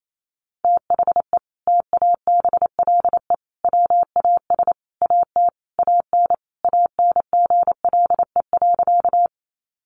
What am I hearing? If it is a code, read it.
Morse code, heard as a keyed tone, about 28 wpm, 700 Hz. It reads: THE TABLE WAS AT AN ANGLE.
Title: Sentences from Top 1500 Words 28wpm